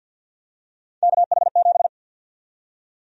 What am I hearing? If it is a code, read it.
Morse code, heard as a keyed tone, about 52 wpm, 700 Hz. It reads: KH7
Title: USA Call Sign Prefixes - Repeated 2x - 50wpm